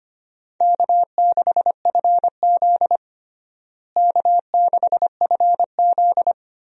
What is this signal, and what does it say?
Morse code, heard as a keyed tone, about 25 wpm, 700 Hz. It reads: K6FZ K6FZ
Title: USA Call Signs - 4 Characters - Repeated 2x - 25wpm